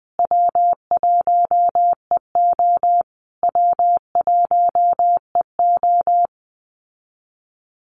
Morse code, heard as a keyed tone, about 20 wpm, 700 Hz. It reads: W1EO W1EO
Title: USA Call Signs Repeated 3x 20wpm